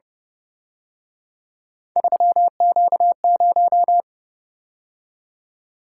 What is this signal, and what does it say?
Morse code, heard as a keyed tone, about 29 wpm, 700 Hz. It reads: 3Q0
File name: Speed-Racing - Three Letter-Number Combinations - 25wpm